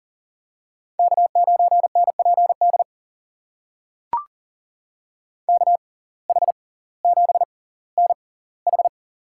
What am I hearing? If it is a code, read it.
Morse code, heard as a keyed tone, about 39 wpm, spaced as for 20 wpm, 700 Hz. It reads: K9NPD KH7NH
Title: USA Call Signs - 5 Characters - 6x Character Spacing - 40wpm